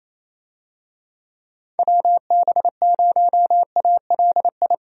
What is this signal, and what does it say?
Morse code, heard as a keyed tone, about 28 wpm, 700 Hz. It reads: WB0ALI
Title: USA Call Signs - 6 Characters - Repeated 2x - 28wpm